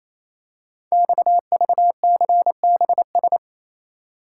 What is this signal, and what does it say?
Morse code, heard as a keyed tone, about 29 wpm, 700 Hz. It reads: XVCBS
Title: Code Groups - 5 Letters - 28wpm